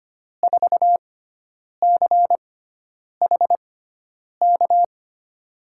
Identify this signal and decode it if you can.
Morse code, heard as a keyed tone, about 25 wpm, spaced as for 9 wpm, 700 Hz. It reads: 4CHK
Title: USA Call Signs - 6x Character Spacing - 25wpm